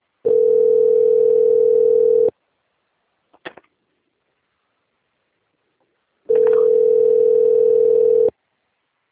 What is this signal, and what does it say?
Ringback tone (North America), 440 + 480 Hz.